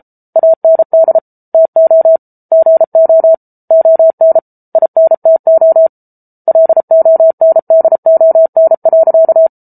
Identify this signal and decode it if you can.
Morse code, heard as a keyed tone, about 34 wpm, 650 Hz. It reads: AND TO GO ON INTO LONDON.